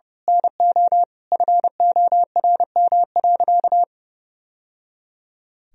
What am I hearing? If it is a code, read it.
Morse code, heard as a keyed tone, about 30 wpm, 700 Hz. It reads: NO FORM.
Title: Sentences from Top 1000 Words - Review Part 1 - 30wpm